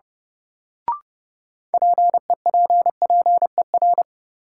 Morse code, heard as a keyed tone, about 30 wpm, 700 Hz. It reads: PEPPER
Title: Sets of 5 Words - New Word Drill - 30wpm